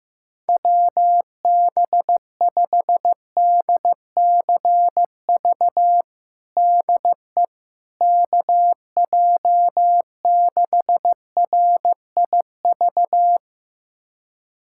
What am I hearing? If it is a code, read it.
Morse code, heard as a keyed tone, about 15 wpm, 700 Hz. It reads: WB5DCV DE KJ6RIV
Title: USA Callsign DE Callsign - 6 Characters - 15wpm